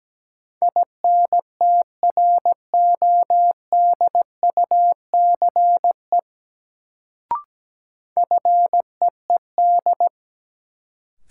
Morse code, heard as a keyed tone, about 17 wpm, 700 Hz. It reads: INTRODUCE FEED
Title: Sentences from Top 1600 Words - New Word Drill - 17wpm